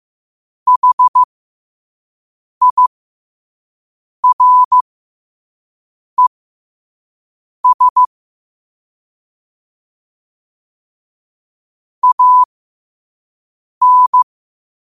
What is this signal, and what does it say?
Morse code, heard as a keyed tone, about 15 wpm, spaced as for 5 wpm, 1000 Hz. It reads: HIRES AN
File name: qotdfiles / 5wpm